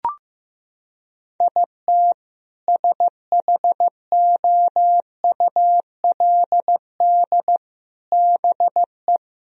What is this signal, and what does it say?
Morse code, heard as a keyed tone, about 16 wpm, 700 Hz. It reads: IT SHOULD BE
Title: Sentences from Top 1000 Words - Review Part 2 - 15wpm